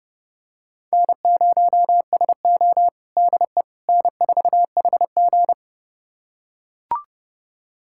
Morse code, heard as a keyed tone, about 30 wpm, 700 Hz. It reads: N0SO DE N4HG
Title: USA Callsign DE Callsign - 4 Characters - 30wpm